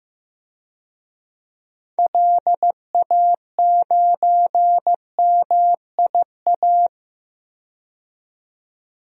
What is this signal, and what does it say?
Morse code, heard as a keyed tone, about 15 wpm, 700 Hz. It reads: LA9MIA
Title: DXCC-100 Call Signs - 6 Characters - 15wpm